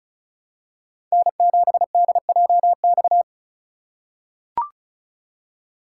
Morse code, heard as a keyed tone, about 36 wpm, 700 Hz. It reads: N7DJX
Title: USA Call Signs - 5 Characters - 4x Character Spacing - 35wpm